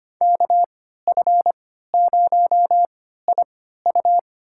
Morse code, heard as a keyed tone, about 25 wpm, spaced as for 14 wpm, 700 Hz. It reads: KF0IU
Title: USA Call Signs - 5 Characters - 3x Character Spacing - 25wpm